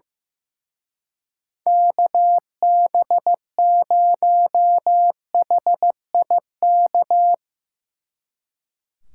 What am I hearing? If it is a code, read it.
Morse code, heard as a keyed tone, about 18 wpm, 700 Hz. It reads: KB0HIK
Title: Speed Racing - USA 2x3 Call Signs 15wpm